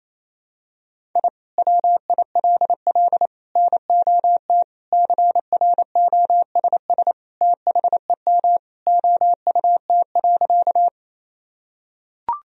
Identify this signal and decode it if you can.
Morse code, heard as a keyed tone, about 28 wpm, 700 Hz. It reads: I WILL NOT CROSS THEM OUT.
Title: Sentences from Top 1000 Words - Review Part 1 - 28wpm